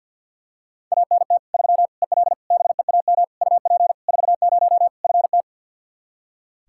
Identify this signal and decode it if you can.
Morse code, heard as a keyed tone, about 45 wpm, 700 Hz. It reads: ANT 3 EL BEAM UP 40 FT
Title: Practice QSO - Exchange 2 - Info - 50wpm